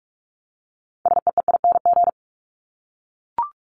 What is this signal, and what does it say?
Morse code, heard as a keyed tone, about 44 wpm, 700 Hz. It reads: SEEING
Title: ING Suffix - Top 100 Words with Base Words 45wpm